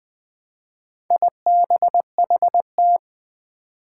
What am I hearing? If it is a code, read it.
Morse code, heard as a keyed tone, about 20 wpm, 700 Hz. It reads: IBHT
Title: Four Challenging Letter-Number Combinations 20wpm